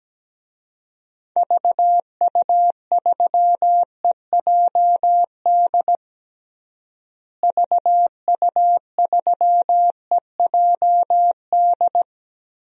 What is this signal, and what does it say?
Morse code, heard as a keyed tone, about 17 wpm, 700 Hz. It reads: VU3EJD VU3EJD
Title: DXCC-100 Call Signs - 6 Characters - Repeated 3x - 17wpm